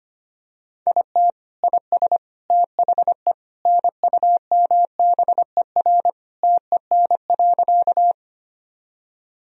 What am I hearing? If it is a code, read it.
Morse code, heard as a keyed tone, about 25 wpm, 700 Hz. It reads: IT IS THE NUMBER TEN.
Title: Sentences from Top 1000 Words - Review Part 2 - 25wpm